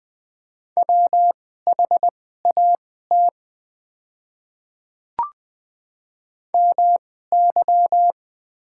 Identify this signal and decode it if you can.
Morse code, heard as a keyed tone, about 20 wpm, spaced as for 14 wpm, 700 Hz. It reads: WHAT MY
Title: 2x Character Spacing - Top 100 Words - 20wpm